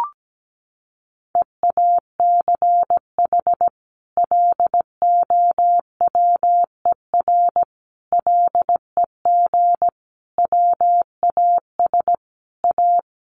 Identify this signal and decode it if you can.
Morse code, heard as a keyed tone, about 17 wpm, 700 Hz. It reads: EACH LOWER LEG WAS A
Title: A Sound Of Thunder 17wpm